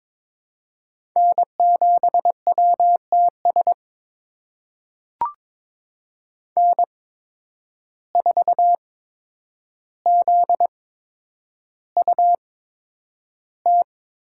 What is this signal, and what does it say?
Morse code, heard as a keyed tone, about 22 wpm, spaced as for 9 wpm, 700 Hz. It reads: N7WTS N4ZUT
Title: USA Call Signs - 5 Characters - 8x Character Spacing - 22wpm